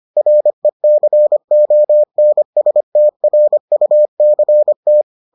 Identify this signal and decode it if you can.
Morse code, heard as a keyed tone, about 25 wpm, 600 Hz. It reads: RECONSTRUCT